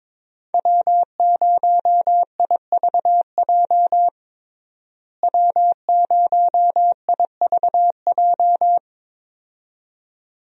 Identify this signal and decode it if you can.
Morse code, heard as a keyed tone, about 22 wpm, 700 Hz. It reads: W0IVJ W0IVJ
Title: USA Call Signs - 5 Characters - Repeated 2x - 22wpm